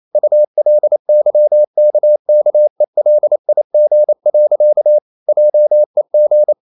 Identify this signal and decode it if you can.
Morse code, heard as a keyed tone, about 28 wpm, 600 Hz. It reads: ULYKKELIG. JEG
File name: Ferie QRQ 28wpm